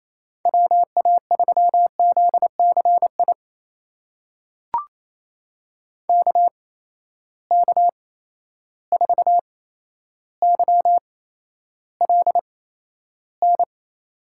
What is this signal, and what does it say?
Morse code, heard as a keyed tone, about 28 wpm, spaced as for 12 wpm, 700 Hz. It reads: WA3ZCI KK4YLN
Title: USA Call Signs - 6 Characters - 8x Character Spacing - 28wpm